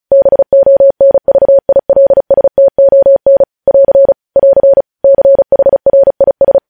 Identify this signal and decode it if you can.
Morse code, heard as a keyed tone, about 36 wpm, 550 Hz. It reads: BONVILSTON <AR> <AR> CHRIS
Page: News Headlines in Morse Code at 35 WPM